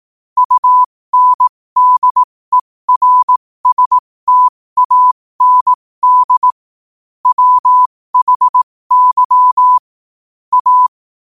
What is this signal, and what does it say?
Morse code, heard as a keyed tone, about 18 wpm, spaced as for 15 wpm, 1000 Hz. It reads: UNDERSTAND WHY A
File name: qotdfiles / 15wpm